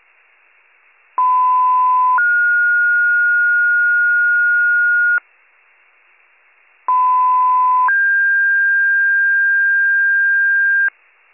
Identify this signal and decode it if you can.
Two-tone page, heard as a one-second tone, then a three-second tone, about 1010 then 1430 Hz, about 1010 then 1600 Hz. It is heard from 1.2 to 10.9 s.